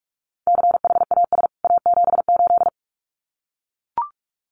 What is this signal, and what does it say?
Morse code, heard as a keyed tone, about 44 wpm, 700 Hz. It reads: CHAS A78
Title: CWT Exchange 45wpm